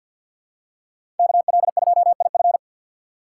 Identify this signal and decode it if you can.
Morse code, heard as a keyed tone, about 52 wpm, 700 Hz. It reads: KL2IF
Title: USA Call Signs - 5 Characters - 2x Character Spacing - 50wpm